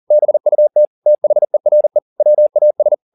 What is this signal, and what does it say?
Morse code, heard as a keyed tone, about 40 wpm, 600 Hz. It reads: BUT THERE WAS